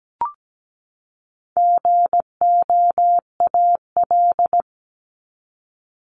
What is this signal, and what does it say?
Morse code, heard as a keyed tone, about 17 wpm, 700 Hz. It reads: GOAL
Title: Top 751-1000 Words 17wpm